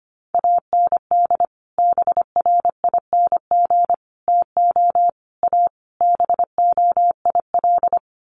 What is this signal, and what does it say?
Morse code, heard as a keyed tone, about 25 wpm, 700 Hz. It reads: AND BRING TO A BOIL
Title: Sets of 5 Words Encore 25wpm